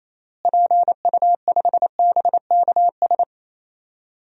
Morse code, heard as a keyed tone, about 28 wpm, 700 Hz. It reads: PU5BKS
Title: DXCC-100 Call Signs - 6 Characters - Repeated 3x - 28wpm